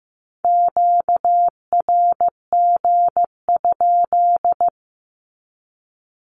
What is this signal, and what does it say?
Morse code, heard as a keyed tone, about 15 wpm, 700 Hz. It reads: QRG?
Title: Q-Codes 15wpm